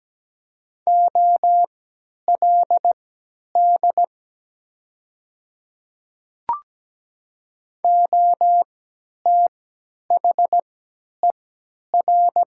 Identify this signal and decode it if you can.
Morse code, heard as a keyed tone, about 17 wpm, spaced as for 10 wpm, 700 Hz. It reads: OLD OTHER
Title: 3x Character Spacing - Top 200 Words - 17wpm